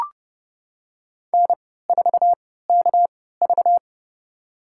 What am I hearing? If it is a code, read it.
Morse code, heard as a keyed tone, about 30 wpm, spaced as for 17 wpm, 700 Hz. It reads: N4KV
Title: USA Call Signs - 4 Characters - 3x Character Spacing - 30wpm